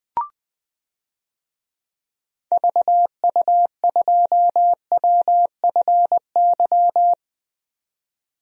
Morse code, heard as a keyed tone, about 20 wpm, 700 Hz. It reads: VU2WFY
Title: DXCC-100 Call Signs - 6 Characters - 20wpm